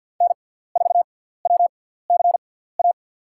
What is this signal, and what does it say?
Morse code, heard as a keyed tone, about 48 wpm, spaced as for 17 wpm, 700 Hz. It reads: N4WCA
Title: USA Call Signs - 5 Characters - 6x Character Spacing - 50wpm